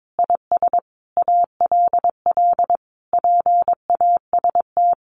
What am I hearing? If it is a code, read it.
Morse code, heard as a keyed tone, about 22 wpm, 700 Hz. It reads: IS ALL PAST
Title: Sentences from Top 600 Words 22wpm